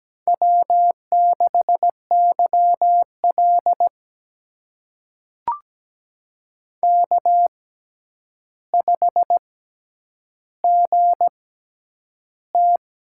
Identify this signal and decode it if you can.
Morse code, heard as a keyed tone, about 17 wpm, spaced as for 9 wpm, 700 Hz. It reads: W6YL K5GT
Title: USA Call Signs - 4 Characters - 6x Character Spacing - 17wpm